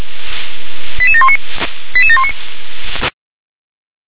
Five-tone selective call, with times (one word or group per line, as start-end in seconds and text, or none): 1.0-1.4 88719
1.9-2.3 88719